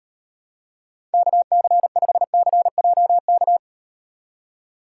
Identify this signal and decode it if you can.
Morse code, heard as a keyed tone, about 36 wpm, 700 Hz. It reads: KC5CJK